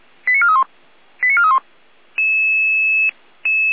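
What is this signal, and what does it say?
Five-tone selective call (ZVEI), heard as quick runs of five tones, tones of about 70 ms. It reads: 89431 89431
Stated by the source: ZVEI five-tone selective call received on an SDR, from a fire brigade.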